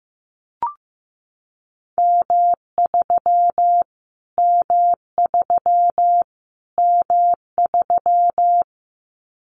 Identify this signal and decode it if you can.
Morse code, heard as a keyed tone, about 15 wpm, 700 Hz. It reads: M3 M3 M3
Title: Callsign Prefixes for Top 25 DXCC - Repeated 3x - 15wpm